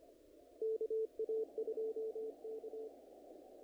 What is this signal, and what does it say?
Morse code, heard as a keyed tone, about 25 wpm, 440 Hz. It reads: KA2K
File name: contests / naqp-cw-k5tr-op-k5pi- -1900